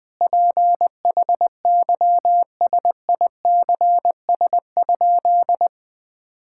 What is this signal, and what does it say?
Morse code, heard as a keyed tone, about 20 wpm, 700 Hz. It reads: PHYSICS?
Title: Sentences from Top 1000 Words - Review Part 5 - 20wpm